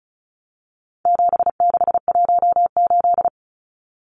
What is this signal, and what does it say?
Morse code, heard as a keyed tone, about 34 wpm, 700 Hz. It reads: 7618
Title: Numbers - 4-Digit 35wpm